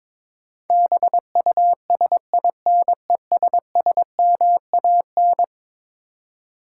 Morse code, heard as a keyed tone, about 22 wpm, 700 Hz. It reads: BUSINESSMAN
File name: practice / 22wpm / Eleven Letter Words - 22wpm